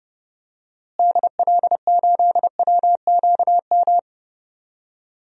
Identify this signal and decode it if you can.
Morse code, heard as a keyed tone, about 29 wpm, 700 Hz. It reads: DL8WQM
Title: DXCC-100 Call Signs - 6 Characters - Repeated 2x - 30wpm